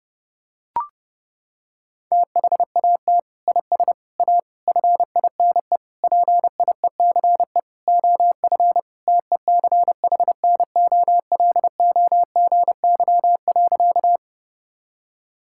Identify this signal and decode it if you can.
Morse code, heard as a keyed tone, about 32 wpm, 700 Hz. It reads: THAT IS A FINE PIECE OF TECHNOLOGY.
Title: Sentences from Top 1000 Words - Review Part 5 - 30wpm